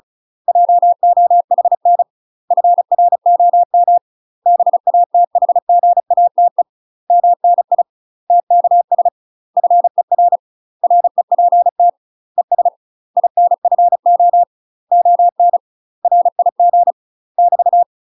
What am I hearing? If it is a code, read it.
Morse code, heard as a keyed tone, about 35 wpm, 700 Hz. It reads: JOHN FROM BATHGATE MNI TKS FER REPT ES INFO ON RIG <BT>
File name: UK MASTER QSOs TRAINING / 35WPM / UK MASTER QSOs 0002